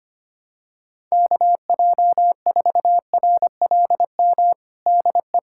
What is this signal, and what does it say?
Morse code, heard as a keyed tone, about 25 wpm, 700 Hz. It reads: KJ4RLM DE
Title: USA Callsign DE Callsign - 6 Characters - 25wpm